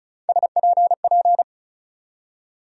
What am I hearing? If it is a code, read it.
Morse code, heard as a keyed tone, about 36 wpm, 700 Hz. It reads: SPP
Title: USA Call Sign Postfixes - 3 Letters 35wpm